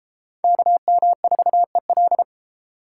Morse code, heard as a keyed tone, about 30 wpm, 700 Hz. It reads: KM4EL